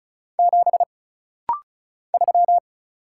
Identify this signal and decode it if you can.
Morse code, heard as a keyed tone, about 36 wpm, 700 Hz. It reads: 7 3
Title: Single Letter-Number Alt - Rapid-Fire 35wpm